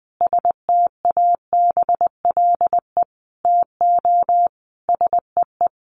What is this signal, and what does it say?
Morse code, heard as a keyed tone, about 20 wpm, 700 Hz. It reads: STABLE TO SEE
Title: Sentences from Top 1500 Words 20wpm